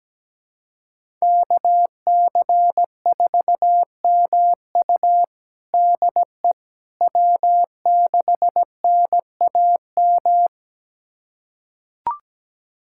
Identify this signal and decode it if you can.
Morse code, heard as a keyed tone, about 17 wpm, 700 Hz. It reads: KC4MU DE W6NAM
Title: USA Callsign DE Callsign - 5 Characters - 17wpm